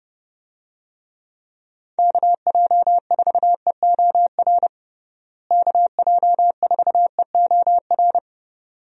Morse code, heard as a keyed tone, about 30 wpm, 700 Hz. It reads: KJ4EOR KJ4EOR
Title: USA Call Signs - 6 Characters - Repeated 2x - 30wpm